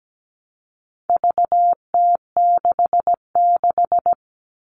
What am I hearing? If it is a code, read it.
Morse code, heard as a keyed tone, about 17 wpm, 700 Hz. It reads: VT66
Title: All Cut Numbers - 4-Digit 17wpm